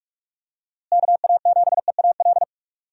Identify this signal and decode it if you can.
Morse code, heard as a keyed tone, about 46 wpm, 700 Hz. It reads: KA7EAL
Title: USA Call Signs - 6 Characters - Repeated 3x - 45wpm